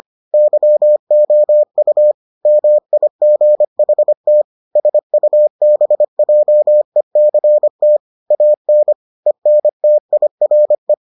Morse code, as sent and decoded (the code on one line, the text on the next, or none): -.-- --- ..- / -- .. --. .... - / ... ..- -... .--- . -.-. - / .- -. / . -. - .. .-. .
YOU MIGHT SUBJECT AN ENTIRE